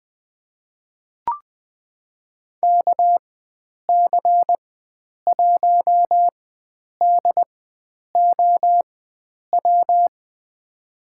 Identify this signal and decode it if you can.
Morse code, heard as a keyed tone, about 20 wpm, spaced as for 9 wpm, 700 Hz. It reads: KC1DOW
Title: USA Call Signs - 6 Characters - 4x Character Spacing - 20wpm